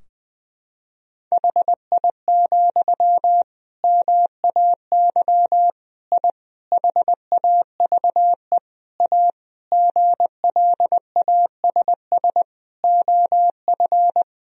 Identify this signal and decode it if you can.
Morse code, heard as a keyed tone, about 21 wpm, 700 Hz. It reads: HI, MAY I HAVE A GLASS OF
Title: Sentences from Top 1000 Words - Review Part 3 - 20wpm